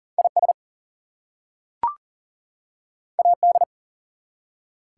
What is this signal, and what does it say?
Morse code, heard as a keyed tone, about 38 wpm, 700 Hz. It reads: IS AD
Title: Two Letter Words - 40wpm